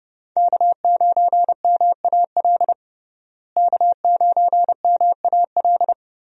Morse code, heard as a keyed tone, about 30 wpm, 700 Hz. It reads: K9MAL K9MAL
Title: USA Call Signs - 5 Characters - Repeated 3x - 30wpm